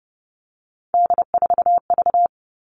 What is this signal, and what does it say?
Morse code, heard as a keyed tone, about 30 wpm, 700 Hz. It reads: D4V
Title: All Cut Numbers - 3-Digit 30wpm